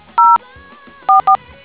DTMF, heard as telephone touch tones, tones of about 100 ms. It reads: 0 44